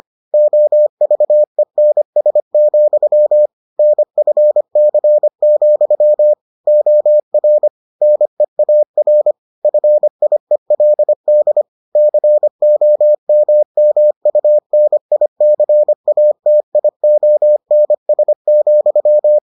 Morse code, heard as a keyed tone, about 25 wpm, 600 Hz. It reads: OVENS, NFC, OR NEAR FIELD COMMUNICATIONS,